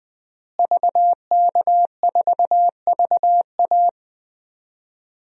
Morse code, heard as a keyed tone, about 20 wpm, 700 Hz. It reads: VK4VA
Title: DXCC-100 Call Signs - 5 Characters - Repeated 3x - 20wpm